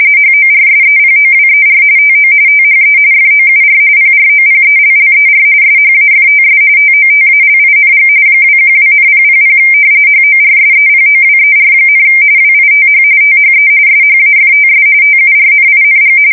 RTTY, about 45.45 baud. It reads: HORSEPOWER TO THE FOUR PROPELLER SHAFTS TOTALING 212,000 HORSEPOWER, WITH THE SHAFTS TURNING
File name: fsk170 / battleshipnj